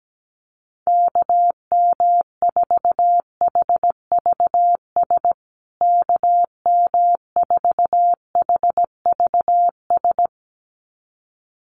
Morse code, heard as a keyed tone, about 17 wpm, 700 Hz. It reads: KM4HVS KM4HVS
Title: USA Call Signs Repeated 2x 17wpm